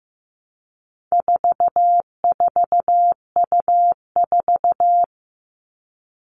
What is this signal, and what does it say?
Morse code, heard as a keyed tone, about 15 wpm, 700 Hz. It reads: 44U4
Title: All Cut Numbers - 4-Digit 15wpm